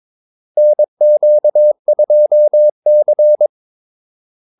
Morse code, heard as a keyed tone, about 22 wpm, 600 Hz. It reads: NQ2C
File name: Callsigns-22wpm